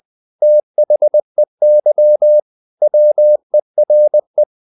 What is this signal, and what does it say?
Morse code, heard as a keyed tone, about 20 wpm, 600 Hz. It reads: THEY WERE
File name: the old man and the sea part 2 20wpm